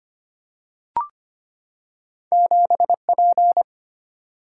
Morse code, heard as a keyed tone, about 25 wpm, 700 Hz. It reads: 7P
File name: Two Letter-Number Combinations - 25wpm